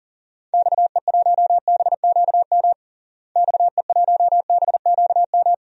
Morse code, heard as a keyed tone, about 40 wpm, 700 Hz. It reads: XE1BQM XE1BQM
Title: DXCC-100 Call Signs - 6 Characters - Repeated 2x - 40wpm